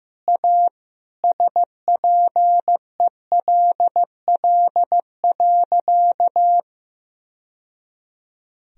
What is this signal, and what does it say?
Morse code, heard as a keyed tone, about 15 wpm, 700 Hz. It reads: A SPELL.
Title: Sentences from Top 1000 Words - Review Part 1 - 15wpm